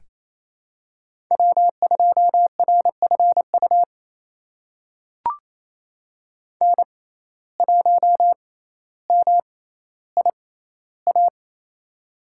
Morse code, heard as a keyed tone, about 28 wpm, spaced as for 14 wpm, 700 Hz. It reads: W2RFU N1MIA
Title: USA Call Signs - 5 Characters - 6x Character Spacing - 28wpm